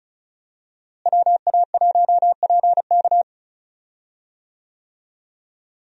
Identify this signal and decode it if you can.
Morse code, heard as a keyed tone, about 34 wpm, 700 Hz. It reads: WA1PK
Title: USA Call Signs - 5 Characters - Repeated 2x - 35wpm